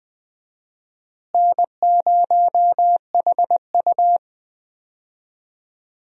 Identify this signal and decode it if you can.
Morse code, heard as a keyed tone, about 21 wpm, 690 Hz. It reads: N0HU
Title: USA Call Signs - 4 Characters - Repeated 3x - 20wpm